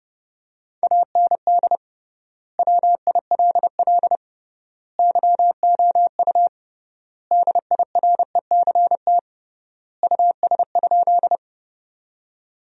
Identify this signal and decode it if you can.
Morse code, heard as a keyed tone, about 30 wpm, 700 Hz. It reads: AND WILL YOU DIRECT US?
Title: 3x Word-Spacing - Sentences from Top 400 Words - 30wpm